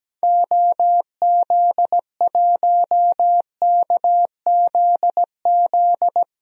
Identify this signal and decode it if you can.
Morse code, heard as a keyed tone, about 17 wpm, 700 Hz. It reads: OZ1KZZ
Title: DXCC-100 Call Signs - 6 Characters - Repeated 2x - 17wpm